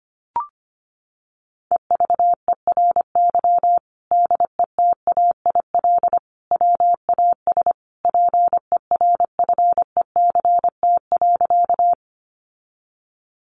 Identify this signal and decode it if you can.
Morse code, heard as a keyed tone, about 25 wpm, 700 Hz. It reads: EVERY DETAIL WAS PERFECT.
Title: Sentences from Top 1500 Words 25wpm